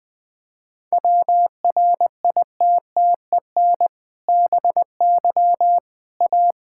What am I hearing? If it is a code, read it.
Morse code, heard as a keyed tone, about 20 wpm, 700 Hz. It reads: WRITTEN BY A
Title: Sentences with 2-Word Phrases - 20wpm